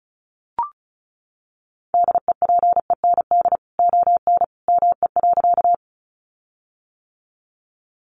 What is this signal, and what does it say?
Morse code, heard as a keyed tone, about 34 wpm, 700 Hz. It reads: DEPEND ON ME.
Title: Sentences from Top 1300 Words 35wpm